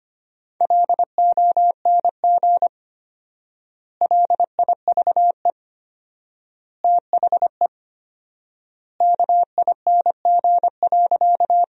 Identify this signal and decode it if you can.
Morse code, heard as a keyed tone, about 25 wpm, 700 Hz. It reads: LONG LIVE THE KING.
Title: 4x Word-Spacing - Sentences from Top 400 Words - 25wpm